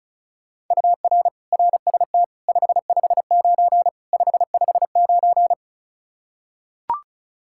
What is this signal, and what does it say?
Morse code, heard as a keyed tone, about 36 wpm, 700 Hz. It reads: UR RST 559 559
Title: Practice QSO - Exchange 1 - Beginning - 35wpm